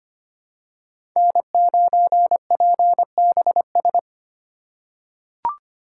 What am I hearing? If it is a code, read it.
Morse code, heard as a keyed tone, about 25 wpm, 700 Hz. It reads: N9PBS
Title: USA Call Signs - 5 Characters - 3x Character Spacing - 25wpm